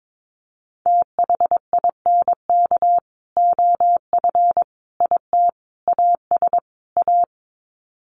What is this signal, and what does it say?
Morse code, heard as a keyed tone, about 22 wpm, 700 Hz. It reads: THINK OF IT AS A
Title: Sets of 5 Words Encore2 22wpm